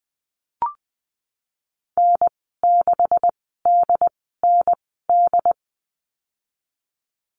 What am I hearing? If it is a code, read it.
Morse code, heard as a keyed tone, about 20 wpm, spaced as for 15 wpm, 700 Hz. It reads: N6DND
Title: 161 Lesson 26g - Letter g - Call Signs 20wpm